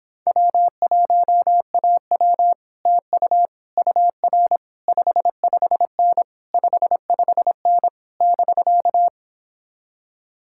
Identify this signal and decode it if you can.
Morse code, heard as a keyed tone, about 26 wpm, 700 Hz. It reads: W1AW TU UR 55N 55N <BK>
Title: Speed-Racing Common QSO Phrases - 22wpm